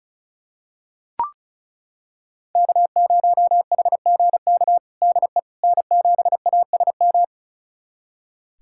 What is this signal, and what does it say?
Morse code, heard as a keyed tone, about 36 wpm, 700 Hz. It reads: K0HGK DE N7ASM
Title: USA Callsign DE Callsign - 5 Characters - 35wpm